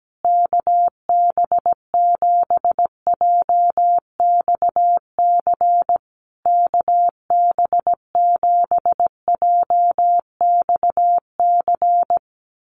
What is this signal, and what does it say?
Morse code, heard as a keyed tone, about 17 wpm, 700 Hz. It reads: KB7JXC KB7JXC
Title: USA Call Signs Repeated 3x 17wpm